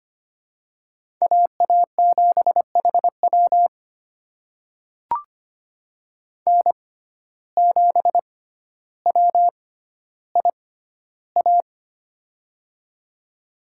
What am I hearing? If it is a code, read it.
Morse code, heard as a keyed tone, about 25 wpm, spaced as for 13 wpm, 700 Hz. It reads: AA7HW N7WIA
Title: USA Call Signs - 5 Characters - 6x Character Spacing - 25wpm